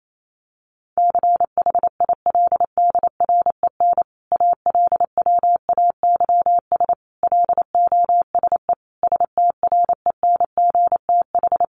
Morse code, heard as a keyed tone, about 28 wpm, 700 Hz. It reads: CHILDREN ALWAYS LOSE STRENGTH
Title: Sentences from Top 1300 Words 28wpm